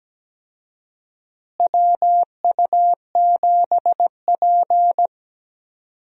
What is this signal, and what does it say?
Morse code, heard as a keyed tone, about 17 wpm, 700 Hz. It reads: WU7P
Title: USA Call Signs - 4 Characters - Repeated 2x - 17wpm